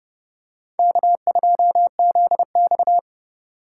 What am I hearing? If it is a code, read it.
Morse code, heard as a keyed tone, about 28 wpm, 700 Hz. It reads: K2ZX